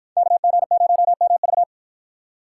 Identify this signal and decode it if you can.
Morse code, heard as a keyed tone, about 48 wpm, 700 Hz. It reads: KG0MV